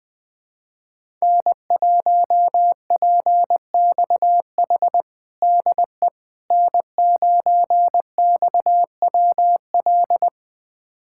Morse code, heard as a keyed tone, about 21 wpm, 700 Hz. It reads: N1PXH DE N9XWL
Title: USA Callsign DE Callsign - 5 Characters - 20wpm